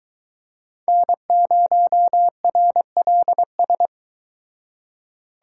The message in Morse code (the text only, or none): N0RLS